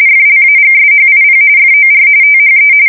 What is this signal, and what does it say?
RTTY, about 45.45 baud. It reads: FLAGSHIP FOR THE
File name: fsk170 / USS-New-Jersey